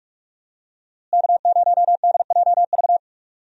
Morse code, heard as a keyed tone, about 42 wpm, 700 Hz. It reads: K0DJV
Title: USA Call Signs - 5 Characters - Repeated 2x - 45wpm